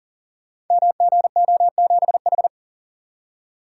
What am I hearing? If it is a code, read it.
Morse code, heard as a keyed tone, about 39 wpm, 700 Hz. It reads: MGO7H
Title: Code Groups - 5 Letter:Numbers - 40wpm